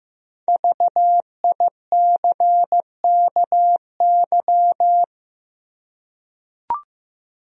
Morse code, heard as a keyed tone, about 15 wpm, 700 Hz. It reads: VICKY
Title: USA Common Names - Max 5 Letters - 15wpm